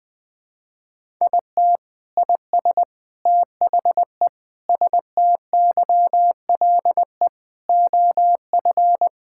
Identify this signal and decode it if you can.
Morse code, heard as a keyed tone, about 20 wpm, 700 Hz. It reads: IT IS THE STYLE OF
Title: Sentences from Top 1000 Words - Review Part 5 - 20wpm